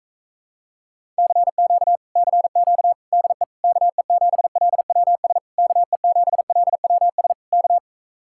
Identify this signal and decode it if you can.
Morse code, heard as a keyed tone, about 45 wpm, 700 Hz. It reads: CQ CQ DE KE7LWS KE7LWS K